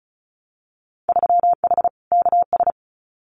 Morse code, heard as a keyed tone, about 36 wpm, 700 Hz. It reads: 3H KS
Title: Winter Field Day Exchanges 35wpm